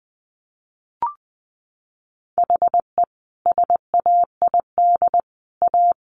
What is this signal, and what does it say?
Morse code, heard as a keyed tone, about 20 wpm, 700 Hz. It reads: HE SAID A
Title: Sentences from Top 600 Words 20wpm